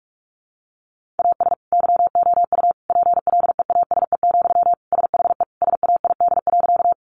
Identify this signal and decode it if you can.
Morse code, heard as a keyed tone, about 46 wpm, 700 Hz. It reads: AS YOU PLEASE, SHE SAID.
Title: Sentences from Top 1500 Words 45wpm